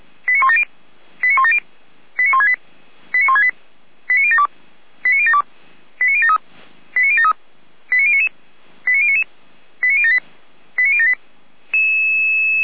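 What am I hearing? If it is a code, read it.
Five-tone selective call (ZVEI), heard as quick runs of five tones, tones of about 70 ms. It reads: 89170 89170 89168 89168 89072 89072 89073 89073 89099 89099 89078 89078
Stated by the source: ZVEI five-tone selective call 89560 received on an SDR.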